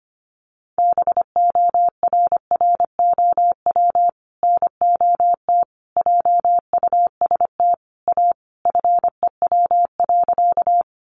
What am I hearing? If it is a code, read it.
Morse code, heard as a keyed tone, about 25 wpm, 700 Hz. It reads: BORROW NOT JUST A FEW.
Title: Sentences from Top 1500 Words 25wpm